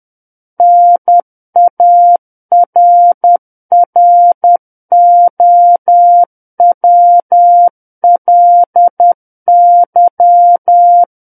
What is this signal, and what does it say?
Morse code, heard as a keyed tone, about 10 wpm, 700 Hz. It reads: NARROWLY
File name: Book One / 01-Chapter 1 - 10 WPM